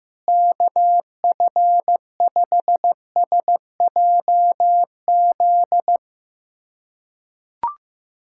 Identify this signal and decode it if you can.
Morse code, heard as a keyed tone, about 15 wpm, 700 Hz. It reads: KF5SJZ
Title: USA Callsign DE Callsign - 6 Characters - 15wpm